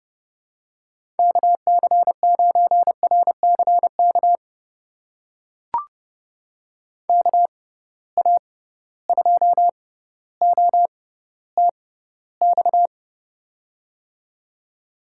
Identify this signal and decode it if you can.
Morse code, heard as a keyed tone, about 30 wpm, spaced as for 15 wpm, 700 Hz. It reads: KC9RCK KA2OTX
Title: USA Call Signs - 6 Characters - 6x Character Spacing - 30wpm